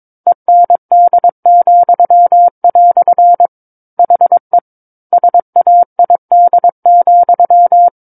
Morse code, heard as a keyed tone, about 22 wpm, 700 Hz. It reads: END," HE SAID,
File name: Book One / 13-Chapter 13 - 22 WPM